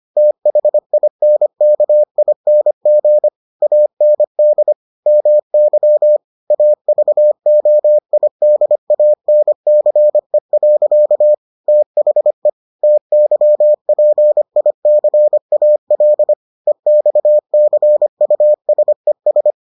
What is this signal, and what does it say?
Morse code, heard as a keyed tone, about 25 wpm, 600 Hz. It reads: THINKING AND MY AVOIDANCE. THE TYPICAL EXCUSES